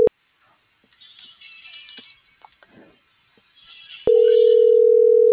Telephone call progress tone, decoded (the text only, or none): ringback tone